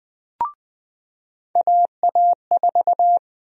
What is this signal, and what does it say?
Morse code, heard as a keyed tone, about 20 wpm, 700 Hz. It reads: AA4
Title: USA Call Sign Prefixes - 20wpm